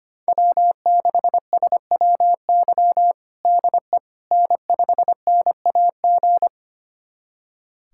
Morse code, heard as a keyed tone, about 25 wpm, 700 Hz. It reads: W6SWY DE N5NAG
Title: USA Callsign DE Callsign 25wpm